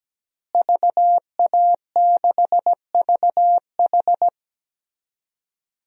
Morse code, heard as a keyed tone, about 17 wpm, 700 Hz. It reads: VA6VH
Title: DXCC-100 Call Signs - 5 Characters - Repeated 3x - 17wpm